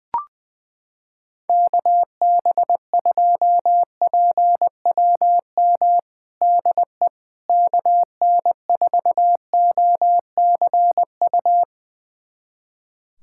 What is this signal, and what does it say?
Morse code, heard as a keyed tone, about 20 wpm, 700 Hz. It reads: KB2PWM DE KN4OCU
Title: USA Callsign DE Callsign - 6 Characters - 20wpm